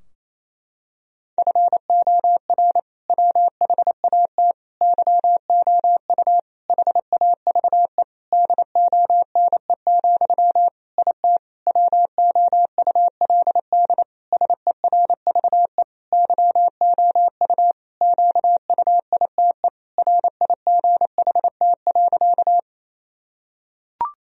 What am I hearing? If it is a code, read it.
Morse code, heard as a keyed tone, about 27 wpm, 700 Hz. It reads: FOR WHAT YOU HAVE DONE, IT WOULD SERVE YOU QUITE RIGHT.
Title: Sentences from Top 1000 Words - Review Part 2 - 28wpm